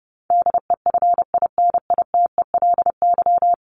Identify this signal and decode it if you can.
Morse code, heard as a keyed tone, about 30 wpm, 700 Hz. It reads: DEFINITELY
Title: Top 1751-2000 Words 30wpm